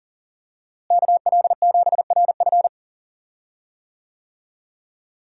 Morse code, heard as a keyed tone, about 40 wpm, 700 Hz. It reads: KL7RF
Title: DXCC-100 Call Signs - 5 Characters - Repeated 2x - 40wpm